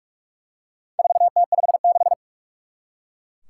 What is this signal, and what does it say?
Morse code, heard as a keyed tone, about 44 wpm, 700 Hz. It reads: 4T56
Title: Four Challenging Letter-Number Combinations 45wpm